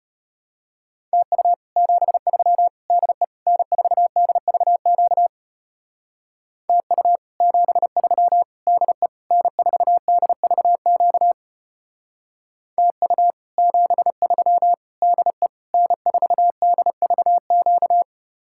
Morse code, heard as a keyed tone, about 33 wpm, 700 Hz. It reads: TU 73 DE N4DVQ TU 73 DE N4DVQ TU 73 DE N4DVQ